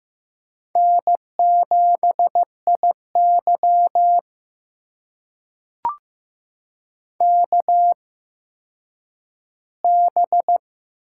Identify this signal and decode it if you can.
Morse code, heard as a keyed tone, about 15 wpm, spaced as for 9 wpm, 700 Hz. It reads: N7IY KB
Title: USA Call Signs - 4 Characters - 8x Character Spacing - 15wpm